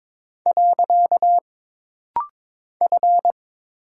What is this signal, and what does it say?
Morse code, heard as a keyed tone, about 22 wpm, 700 Hz. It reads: . F
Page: This rapid-fire timing is geared to push your ICR (Instant Character Recognition) skills.